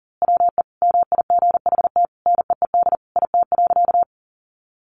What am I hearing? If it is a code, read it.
Morse code, heard as a keyed tone, about 40 wpm, 700 Hz. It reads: WE MIGHT NEED IT.
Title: A Sound Of Thunder 40wpm